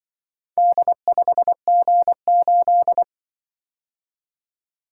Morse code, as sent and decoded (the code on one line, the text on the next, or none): -.. ..... --. ---..
D5G8